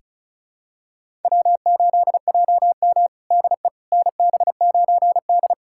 Morse code, heard as a keyed tone, about 36 wpm, 700 Hz. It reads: W8JM DE NB9D
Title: USA Callsign DE Callsign - 4 Characters - 35wpm